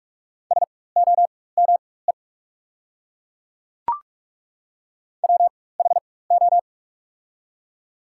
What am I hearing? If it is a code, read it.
Morse code, heard as a keyed tone, about 46 wpm, spaced as for 21 wpm, 690 Hz. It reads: SOME WHO
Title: 4x Character Spacing - Top 200 Words - 45wpm